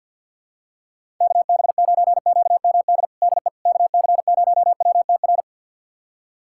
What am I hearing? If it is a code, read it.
Morse code, heard as a keyed tone, about 52 wpm, 700 Hz. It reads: KB9QMD DE KC0WTR
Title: USA Callsign DE Callsign 50wpm